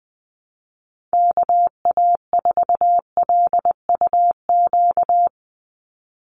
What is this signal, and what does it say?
Morse code, heard as a keyed tone, about 20 wpm, 700 Hz. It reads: KA4LUQ
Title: USA Call Signs 20wpm